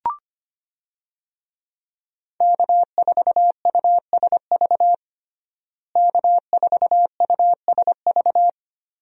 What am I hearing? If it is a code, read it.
Morse code, heard as a keyed tone, about 25 wpm, 700 Hz. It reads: K4USV K4USV
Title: USA Call Signs - 5 Characters - Repeated 2x - 25wpm